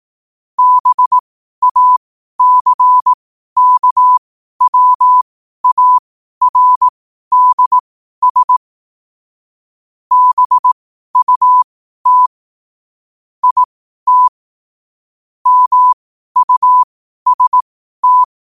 Morse code, heard as a keyed tone, about 18 wpm, spaced as for 12 wpm, 1000 Hz. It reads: BACKWARDS BUT IT MUST
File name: qotdfiles / 12wpm